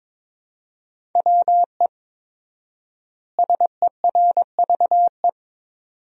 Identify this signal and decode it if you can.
Morse code, heard as a keyed tone, about 22 wpm, 700 Hz. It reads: WE SERVE
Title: 4x Word-Spacing - Sentences from Top 400 Words - 22wpm